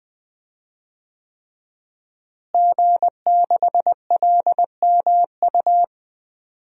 Morse code, heard as a keyed tone, about 20 wpm, 700 Hz. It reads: G6LMU
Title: DXCC-100 Call Signs - 5 Characters - Repeated 3x - 20wpm